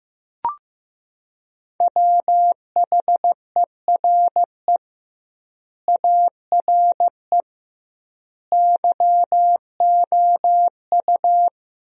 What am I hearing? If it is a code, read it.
Morse code, heard as a keyed tone, about 15 wpm, 700 Hz. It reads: WHERE ARE YOU
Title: 2x Word-Spacing - Sentences from Top 200 Words - 15wpm